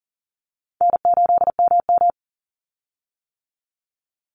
Morse code, heard as a keyed tone, about 39 wpm, 700 Hz. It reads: N8MM
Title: USA Call Signs 40wpm